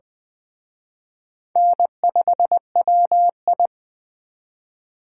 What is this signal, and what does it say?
Morse code, heard as a keyed tone, about 20 wpm, 700 Hz. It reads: N5WI
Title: Speed Racing - USA 1x2 Call Signs 17wpm